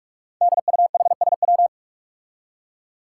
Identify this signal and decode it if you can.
Morse code, heard as a keyed tone, about 44 wpm, 700 Hz. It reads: DUHSW
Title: Five Challenging Letter-Number Combinations 45wpm